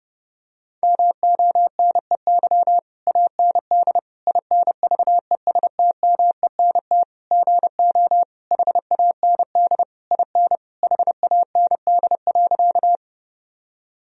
Morse code, heard as a keyed tone, about 32 wpm, 700 Hz. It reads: MONEY AND INVESTMENT GO HAND IN HAND.
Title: Sentences from Top 1000 Words - Review Part 4 - 30wpm